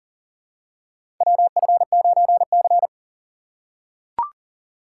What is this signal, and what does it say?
Morse code, heard as a keyed tone, about 40 wpm, 700 Hz. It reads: WF9C
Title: USA Call Signs - 4 Characters - 4x Character Spacing - 40wpm